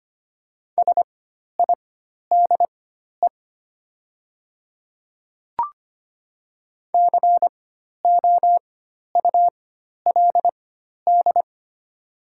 Morse code, heard as a keyed tone, about 25 wpm, spaced as for 12 wpm, 700 Hz. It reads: SIDE COULD
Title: 4x Character Spacing - Top 200 Words - 25wpm